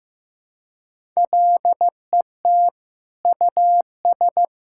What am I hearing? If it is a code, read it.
Morse code, heard as a keyed tone, about 15 wpm, 700 Hz. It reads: LET US
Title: Sentences from Top 1000 Words - Review Part 6 - 15wpm